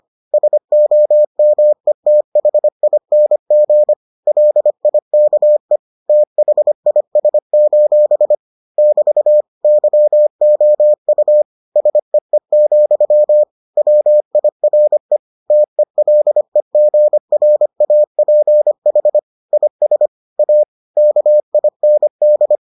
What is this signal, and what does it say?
Morse code, heard as a keyed tone, about 25 wpm, 600 Hz. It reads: SOMETHING LIKE THIS: <BT> YOU SEE, WIRE TELEGRAPH IS A KIND